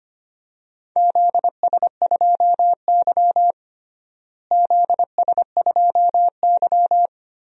Morse code, heard as a keyed tone, about 25 wpm, 700 Hz. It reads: ZS2Y ZS2Y
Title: DXCC-100 Call Signs - 4 Characters - Repeated 2x - 25wpm